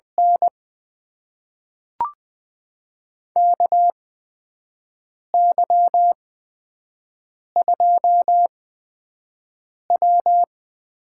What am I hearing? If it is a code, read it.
Morse code, heard as a keyed tone, about 20 wpm, spaced as for 5 wpm, 700 Hz. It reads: N KY2W
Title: USA Call Signs - 4 Characters - 8x Character Spacing - 20wpm